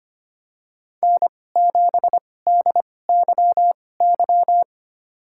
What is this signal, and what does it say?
Morse code, heard as a keyed tone, about 25 wpm, spaced as for 18 wpm, 700 Hz. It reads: N7DYY
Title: USA Call Signs - 5 Characters - 2x Character Spacing - 25wpm